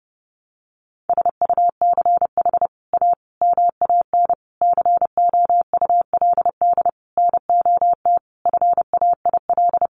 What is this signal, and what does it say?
Morse code, heard as a keyed tone, about 28 wpm, 700 Hz. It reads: SUCH A MAN COULD NOT FAIL
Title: Sentences from Top 1100 Words 30wpm